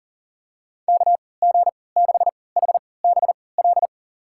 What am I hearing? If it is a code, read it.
Morse code, heard as a keyed tone, about 40 wpm, spaced as for 23 wpm, 700 Hz. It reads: KG6HBL
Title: USA Call Signs - 6 Characters - 3x Character Spacing - 40wpm